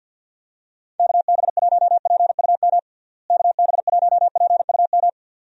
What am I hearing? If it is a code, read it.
Morse code, heard as a keyed tone, about 50 wpm, 700 Hz. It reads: KB1PUM KB1PUM
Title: USA Call Signs - 6 Characters - Repeated 2x - 50wpm